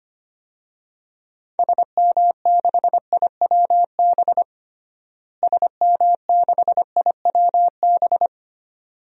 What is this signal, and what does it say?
Morse code, heard as a keyed tone, about 25 wpm, 700 Hz. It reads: SM6IWB SM6IWB
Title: DXCC-100 Call Signs - 6 Characters - Repeated 2x - 25wpm